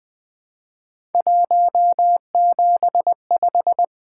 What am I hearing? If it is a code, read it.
Morse code, heard as a keyed tone, about 20 wpm, 700 Hz. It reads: 175
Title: Three Letter-Number Combinations - 20wpm